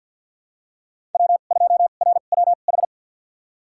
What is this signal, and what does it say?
Morse code, heard as a keyed tone, about 51 wpm, spaced as for 36 wpm, 700 Hz. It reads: W2RWH
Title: USA Call Signs - 5 Characters - 2x Character Spacing - 50wpm